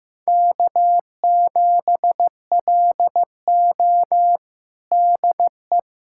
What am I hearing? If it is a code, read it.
Morse code, heard as a keyed tone, about 15 wpm, 700 Hz. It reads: K7LO DE
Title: USA Callsign DE Callsign - 4 Characters - 15wpm